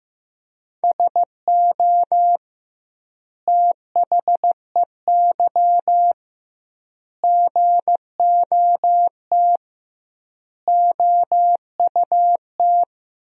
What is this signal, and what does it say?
Morse code, heard as a keyed tone, about 16 wpm, 700 Hz. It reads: SO THEY GOT OUT
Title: 2x Word-Spacing - Sentences from Top 300 Words - 15wpm